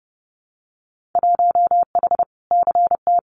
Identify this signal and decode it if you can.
Morse code, heard as a keyed tone, about 30 wpm, 700 Hz. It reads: 1H CT
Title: Winter Field Day Exchanges 30wpm